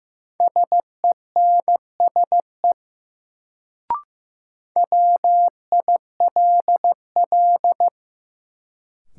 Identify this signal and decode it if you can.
Morse code, heard as a keyed tone, about 15 wpm, 700 Hz. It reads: SENSE WILL
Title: Sets of 2 Words - New Word Drill - 15wpm